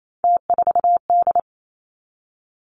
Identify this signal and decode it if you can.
Morse code, heard as a keyed tone, about 28 wpm, 700 Hz. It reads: T4D
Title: Three Challenging Letter-Number Combinations 28wpm